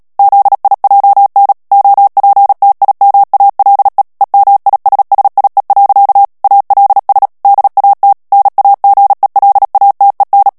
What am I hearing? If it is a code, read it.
Morse code, heard as a keyed tone, about 37 wpm, 800 Hz. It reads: ZIJN OPTIMALE EMISSIE. ALS DAT NAGELATEN
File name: morse mp3 / HOLLANDAIS texte clair 2 vitesse 35 mots minute